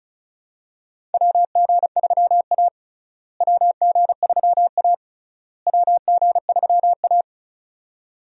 Morse code, heard as a keyed tone, about 34 wpm, 700 Hz. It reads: WG3A WG3A WG3A
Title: USA Call Signs - 4 Characters - Repeated 3x - 35wpm